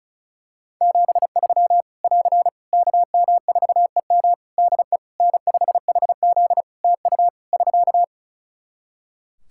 Morse code, heard as a keyed tone, about 34 wpm, 700 Hz. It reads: 73 <AR> KM4EM DE N5HZ TU <SK>
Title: Practice QSO - Exchange 4 - Ending - 35wpm